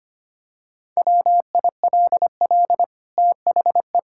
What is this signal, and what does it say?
Morse code, heard as a keyed tone, about 25 wpm, 700 Hz. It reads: WILL THE
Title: Sentences from Top 1000 Words - Review Part 5 - 25wpm